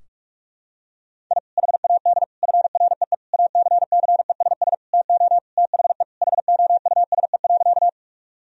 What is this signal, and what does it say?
Morse code, heard as a keyed tone, about 46 wpm, 700 Hz. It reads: I HAD FREE ACCESS TO THE HOUSE.
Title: Sentences from Top 1000 Words - Review Part 4 - 45wpm